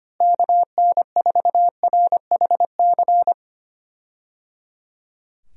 Morse code, heard as a keyed tone, about 25 wpm, 700 Hz. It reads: KN4RHC
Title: USA Call Signs - 6 Characters - Repeated 3x - 25wpm